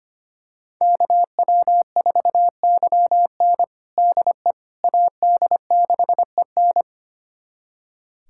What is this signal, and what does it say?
Morse code, heard as a keyed tone, about 25 wpm, 700 Hz. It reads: KW4YN DE AD6EN
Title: USA Callsign DE Callsign - 5 Characters - 25wpm